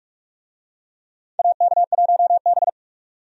Morse code, heard as a keyed tone, about 45 wpm, 700 Hz. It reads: AK1B
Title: USA Call Signs - 4 Characters - 45wpm